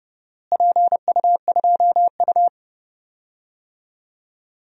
Morse code, heard as a keyed tone, about 32 wpm, 700 Hz. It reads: PU2U
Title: DXCC-100 Call Signs - 4 Characters - Repeated 3x - 30wpm